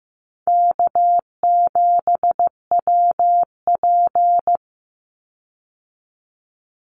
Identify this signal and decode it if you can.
Morse code, heard as a keyed tone, about 15 wpm, 700 Hz. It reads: K7WP
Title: Easy Call Signs Repeated 3x - 15wpm